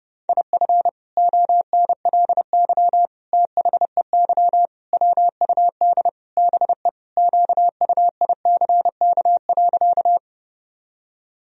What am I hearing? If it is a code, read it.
Morse code, heard as a keyed tone, about 30 wpm, 700 Hz. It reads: IF ONLY THEY WUD BE QUICK.
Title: Sentences from Top 500 Words and Abbreviations 30wpm